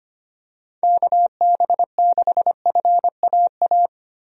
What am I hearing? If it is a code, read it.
Morse code, heard as a keyed tone, about 25 wpm, 700 Hz. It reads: KB6FAA
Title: USA Call Signs - 6 Characters - 25wpm